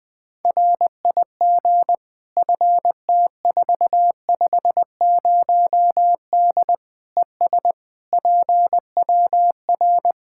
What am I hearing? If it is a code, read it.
Morse code, heard as a keyed tone, about 21 wpm, 700 Hz. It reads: RIG FT450D ES PWR
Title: Practice QSO - Exchange 2 - Info - 20wpm